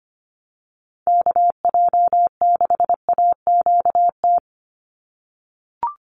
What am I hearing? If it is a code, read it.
Morse code, heard as a keyed tone, about 25 wpm, 700 Hz. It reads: KJ6AQT
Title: USA Call Signs 25wpm